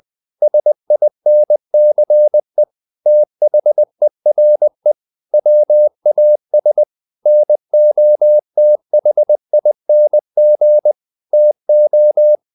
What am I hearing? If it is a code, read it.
Morse code, heard as a keyed tone, about 20 wpm, 600 Hz. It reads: SINCE THERE WAS NOTHING TO
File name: the old man and the sea part 2 20wpm